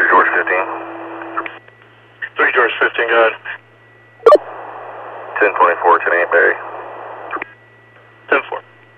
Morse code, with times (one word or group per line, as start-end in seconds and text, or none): none